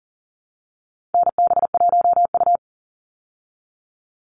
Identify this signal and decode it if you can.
Morse code, heard as a keyed tone, about 40 wpm, 700 Hz. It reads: NB1U
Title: Call Signs Repeated 2x - 40wpm